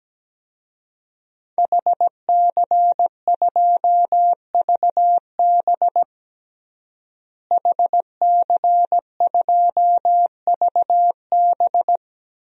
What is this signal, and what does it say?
Morse code, heard as a keyed tone, about 17 wpm, 700 Hz. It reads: HC2VB HC2VB
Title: DXCC-100 Call Signs - 5 Characters - Repeated 2x - 17wpm